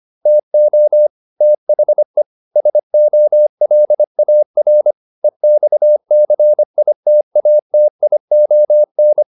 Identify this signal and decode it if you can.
Morse code, heard as a keyed tone, about 25 wpm, 600 Hz. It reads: TO THE SOLAR EXCITATION
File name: SS211 25